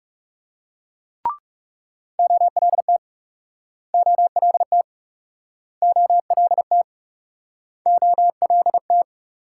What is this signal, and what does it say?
Morse code, heard as a keyed tone, about 38 wpm, 700 Hz. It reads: OLT OLT OLT OLT